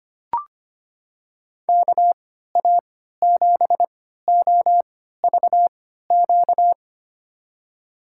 Morse code, heard as a keyed tone, about 25 wpm, spaced as for 14 wpm, 700 Hz. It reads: KA7OVQ
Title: USA Call Signs - 6 Characters - 3x Character Spacing - 25wpm